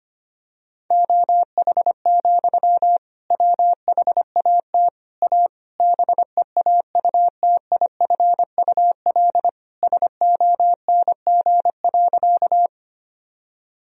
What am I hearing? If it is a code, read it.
Morse code, heard as a keyed tone, about 25 wpm, 700 Hz. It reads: OH, WHAT A BEAUTIFUL SONG.
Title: Sentences from Top 1000 Words - Review Part 2 - 25wpm